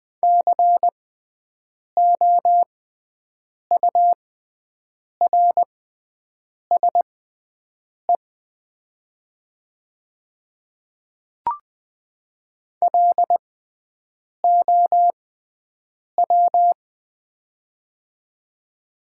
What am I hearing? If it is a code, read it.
Morse code, heard as a keyed tone, about 20 wpm, spaced as for 7 wpm, 700 Hz. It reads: COURSE LOW
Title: 6x Character Spacing - Top 500 Words - 20wpm